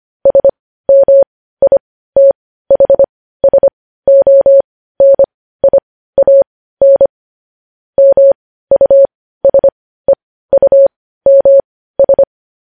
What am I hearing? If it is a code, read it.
Morse code, heard as a keyed tone, about 25 wpm, spaced as for 15 wpm, 550 Hz. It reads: SMITHSONIAN MUSEUMS
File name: podcast / News headlines-1x-25x15